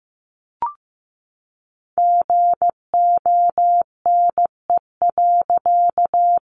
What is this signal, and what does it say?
Morse code, heard as a keyed tone, about 15 wpm, 700 Hz. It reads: GONE.
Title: A Sound Of Thunder 15wpm